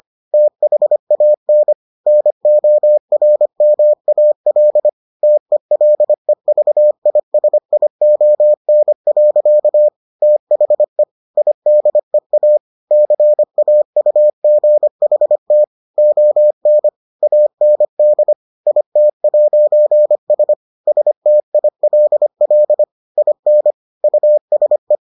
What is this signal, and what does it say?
Morse code, heard as a keyed tone, about 25 wpm, 600 Hz. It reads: THAN NORMAL TELEVISION. THE IDEA CAUGHT ON AND IT'S STILL IN USE